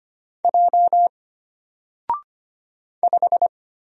Morse code, heard as a keyed tone, about 25 wpm, 700 Hz. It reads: J 5
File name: Single-Letter-Number-25wpm